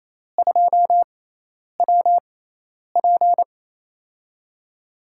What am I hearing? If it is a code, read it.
Morse code, heard as a keyed tone, about 28 wpm, spaced as for 10 wpm, 700 Hz. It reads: 2WP
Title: USA Call Signs - 4 Characters - 6x Character Spacing - 28wpm